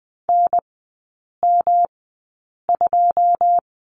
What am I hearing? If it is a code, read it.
Morse code, heard as a keyed tone, about 20 wpm, spaced as for 8 wpm, 700 Hz. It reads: NM2
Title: 107 Lesson 18e - Letter m - Three Characters 20wpm